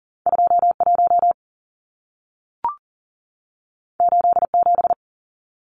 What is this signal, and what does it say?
Morse code, heard as a keyed tone, about 40 wpm, 700 Hz. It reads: 21 87
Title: Numbers - 2-Digit 40wpm